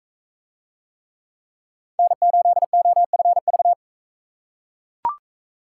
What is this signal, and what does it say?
Morse code, heard as a keyed tone, about 41 wpm, 700 Hz. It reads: N8OFV